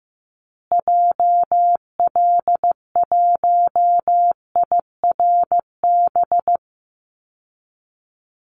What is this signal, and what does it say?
Morse code, heard as a keyed tone, about 15 wpm, 700 Hz. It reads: JL1IRB
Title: Call Signs Repeated 3x - 15wpm